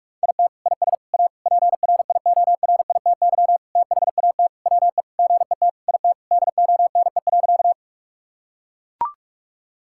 Morse code, heard as a keyed tone, about 42 wpm, 700 Hz. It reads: IT IS A PRIORITY THAT WE GET IT DONE.
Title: Sentences from Top 1600 Words 45wpm